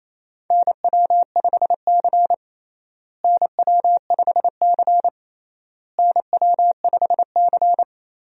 Morse code, heard as a keyed tone, about 28 wpm, 700 Hz. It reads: NW5C NW5C NW5C
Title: USA Call Signs - 4 Characters - Repeated 3x - 28wpm